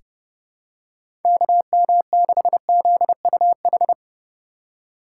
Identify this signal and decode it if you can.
Morse code, heard as a keyed tone, about 30 wpm, 700 Hz. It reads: KM6ZUH
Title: USA Call Signs - 6 Characters - 2x Character Spacing - 30wpm